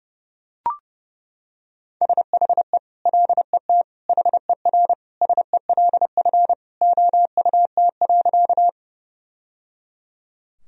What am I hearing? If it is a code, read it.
Morse code, heard as a keyed tone, about 30 wpm, 700 Hz. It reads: SHE LET HERSELF OUT.
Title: Sentences from Top 1000 Words - Review Part 1 - 30wpm